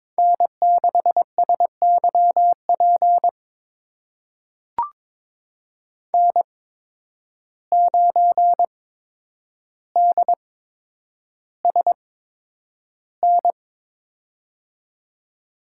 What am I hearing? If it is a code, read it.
Morse code, heard as a keyed tone, about 22 wpm, spaced as for 9 wpm, 700 Hz. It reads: N6SYP N9DSN
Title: USA Call Signs - 8x Character Spacing - 22wpm